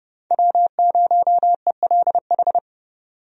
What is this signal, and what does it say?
Morse code, heard as a keyed tone, about 30 wpm, 700 Hz. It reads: W0ELH
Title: USA Call Signs - 5 Characters - 30wpm